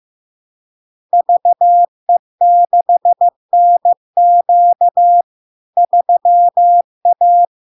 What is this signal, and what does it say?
Morse code, heard as a keyed tone, about 15 wpm, 700 Hz. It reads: VE6NQ 3A
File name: LARK-FD-2022-Log-in-CW-15wpm-700-Hz